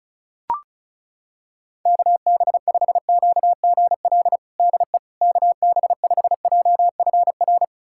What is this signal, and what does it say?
Morse code, heard as a keyed tone, about 34 wpm, 690 Hz. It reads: KB5QGL DE KB5JFR
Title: USA Callsign DE Callsign - 6 Characters - 35wpm